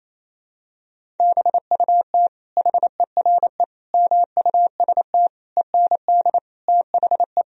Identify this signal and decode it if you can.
Morse code, heard as a keyed tone, about 27 wpm, 700 Hz. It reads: BUT HERE MUST END THE
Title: Sentences from Top 1000 Words - Review Part 1 - 28wpm